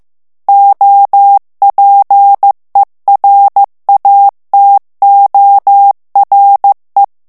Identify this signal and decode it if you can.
Morse code, heard as a keyed tone, about 15 wpm, 800 Hz. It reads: OPERATORE
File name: morse mp3 / ITALIEN texte clair 2 vitesse 14 mots minute